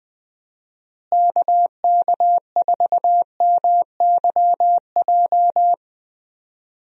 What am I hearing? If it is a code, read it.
Morse code, heard as a keyed tone, about 20 wpm, 700 Hz. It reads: KK4MYJ
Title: USA Call Signs - 6 Characters - Repeated 2x - 20wpm